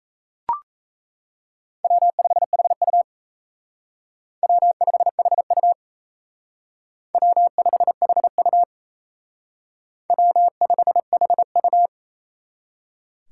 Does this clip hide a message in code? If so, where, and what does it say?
Yes: Morse code from 1.8 to 11.9 s, heard as a keyed tone, about 37 wpm, 700 Hz. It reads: W5HU W5HU W5HU W5HU